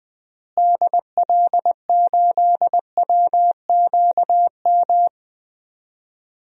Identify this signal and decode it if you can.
Morse code, heard as a keyed tone, about 21 wpm, 700 Hz. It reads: DL8WQM
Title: DXCC-100 Call Signs - 6 Characters - Repeated 2x - 20wpm